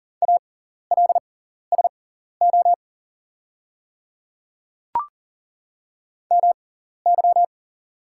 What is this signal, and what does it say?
Morse code, heard as a keyed tone, about 41 wpm, spaced as for 14 wpm, 690 Hz. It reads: ALSO MY
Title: 6x Character Spacing - Top 500 Words - 40wpm